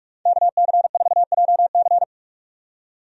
Morse code, heard as a keyed tone, about 46 wpm, 700 Hz. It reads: KC4JC
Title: USA Call Signs - 5 Characters - 45wpm